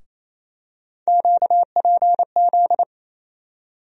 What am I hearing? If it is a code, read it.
Morse code, heard as a keyed tone, about 28 wpm, 700 Hz. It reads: QPZ
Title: USA Call Sign Postfixes - 3 Letters 28wpm